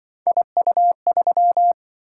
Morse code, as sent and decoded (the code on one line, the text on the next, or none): .. ..- ...--
IU3